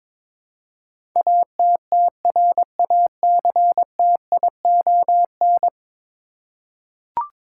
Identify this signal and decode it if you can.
Morse code, heard as a keyed tone, about 22 wpm, 700 Hz. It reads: ATTRACTION
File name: practice / 22wpm / 10 Letter Words - First 3 Letters Given - 22wpm